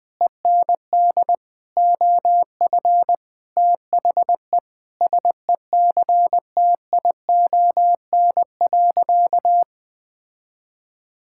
Morse code, heard as a keyed tone, about 20 wpm, 700 Hz. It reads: END OF THE SECTION.
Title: Sentences from Top 1000 Words - Review Part 5 - 20wpm